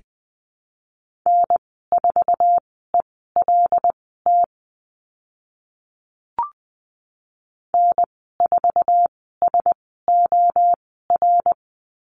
Morse code, heard as a keyed tone, about 20 wpm, spaced as for 14 wpm, 700 Hz. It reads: N4ELT N4SOR
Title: 088 Lesson 15g - Number 5 - Call Signs 20wpm